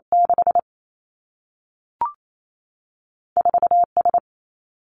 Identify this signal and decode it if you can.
Morse code, heard as a keyed tone, about 28 wpm, 700 Hz. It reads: 6 4 S
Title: Two Challenging Letter-Number Combinations 28wpm